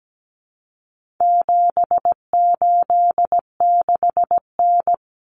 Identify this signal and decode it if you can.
Morse code, heard as a keyed tone, about 17 wpm, 700 Hz. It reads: 786N
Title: Common Cut Numbers - 4-Digit 17wpm